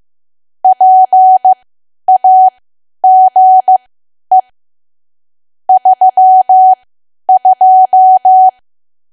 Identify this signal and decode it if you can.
Morse code, heard as a keyed tone, about 15 wpm, spaced as for 10 wpm, 750 Hz. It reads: PAGE 32
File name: cwa / W1AW-Level2 / 10WPM